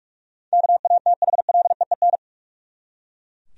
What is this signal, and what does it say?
Morse code, heard as a keyed tone, about 42 wpm, 700 Hz. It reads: KATHLEEN
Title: USA Common Names - 45wpm